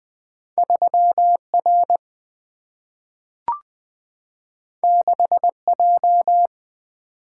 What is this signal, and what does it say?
Morse code, heard as a keyed tone, about 20 wpm, 700 Hz. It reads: 3 R 6 J
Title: Single-Letter Single-Number 20wpm